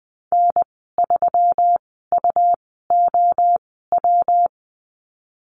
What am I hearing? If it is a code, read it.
Morse code, heard as a keyed tone, about 20 wpm, spaced as for 15 wpm, 700 Hz. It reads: N3UOW
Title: 122 Lesson 20g - Number 3 - Call Signs 20wpm